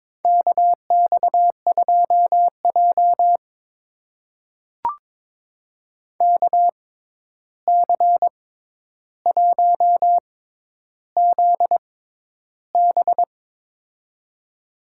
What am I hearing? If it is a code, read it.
Morse code, heard as a keyed tone, about 22 wpm, spaced as for 11 wpm, 700 Hz. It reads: KX2J KC1ZB
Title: USA Call Signs - 6x Character Spacing - 22wpm